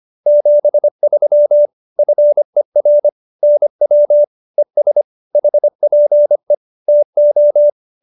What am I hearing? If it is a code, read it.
Morse code, heard as a keyed tone, about 25 wpm, 590 Hz. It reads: 73 FER NW ES HPE TO